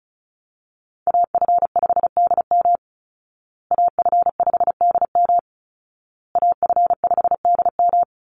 Morse code, heard as a keyed tone, about 34 wpm, 700 Hz. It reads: AF5DM AF5DM AF5DM
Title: Call Signs Repeated 2x - 35wpm